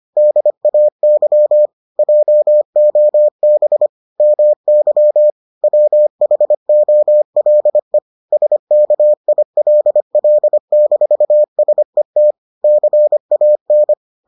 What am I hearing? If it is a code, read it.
Morse code, heard as a keyed tone, about 25 wpm, 600 Hz. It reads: DAY JOB MY WHOLE SKILL-SET CAN